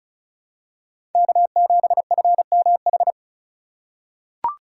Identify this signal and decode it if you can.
Morse code, heard as a keyed tone, about 36 wpm, 700 Hz. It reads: K7FMH
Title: USA Call Signs - 5 Characters - 2x Character Spacing - 35wpm